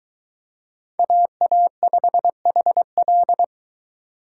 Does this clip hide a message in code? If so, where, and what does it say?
Yes: Morse code from 1.0 to 3.4 s, heard as a keyed tone, about 23 wpm, 700 Hz. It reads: AA5HL